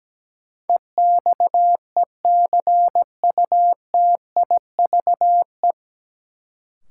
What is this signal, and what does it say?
Morse code, heard as a keyed tone, about 17 wpm, 700 Hz. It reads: EXECUTIVE
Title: Sets of 5 Words Encore - New Word Drill - 17wpm